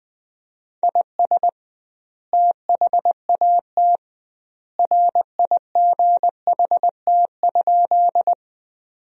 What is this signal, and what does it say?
Morse code, heard as a keyed tone, about 21 wpm, 700 Hz. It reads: IS THAT RIGHT?
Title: 2x Word-Spacing - Sentences from Top 200 Words - 20wpm